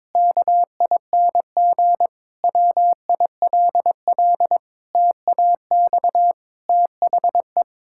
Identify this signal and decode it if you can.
Morse code, heard as a keyed tone, about 22 wpm, 700 Hz. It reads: KING WILL TAX THE
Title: Sentences from Top 1000 Words - Review Part 6 - 22wpm